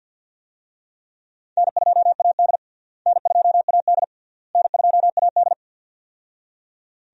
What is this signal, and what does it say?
Morse code, heard as a keyed tone, about 46 wpm, 700 Hz. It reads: N2AD N2AD N2AD
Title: USA Call Signs - 4 Characters - Repeated 3x - 50wpm